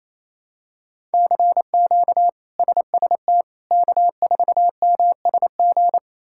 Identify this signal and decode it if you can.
Morse code, heard as a keyed tone, about 28 wpm, 700 Hz. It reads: CQ SST K4MSG
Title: K1USN SST QSO Practice - 28wpm